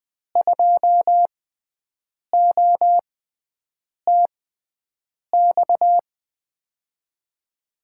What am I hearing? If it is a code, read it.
Morse code, heard as a keyed tone, about 20 wpm, spaced as for 7 wpm, 700 Hz. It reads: 2OTX
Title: USA Call Signs - 6 Characters - 6x Character Spacing - 20wpm